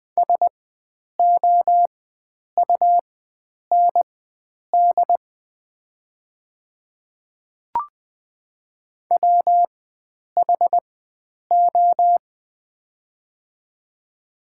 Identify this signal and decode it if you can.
Morse code, heard as a keyed tone, about 20 wpm, spaced as for 9 wpm, 700 Hz. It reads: SOUND WHO
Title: 4x Character Spacing - Top 100 Words - 20wpm